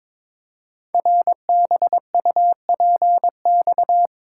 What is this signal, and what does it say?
Morse code, heard as a keyed tone, about 22 wpm, 700 Hz. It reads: RBUPX
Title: Code Groups - 5 Letters - 22wpm